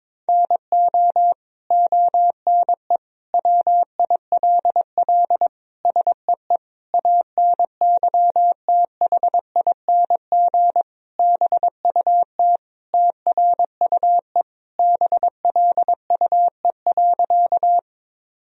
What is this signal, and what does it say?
Morse code, heard as a keyed tone, about 23 wpm, 700 Hz. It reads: NO ONE WILL SEE ANYTHING BUT TRUE BLUE.
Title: Sentences from Top 1000 Words - Review Part 3 - 22wpm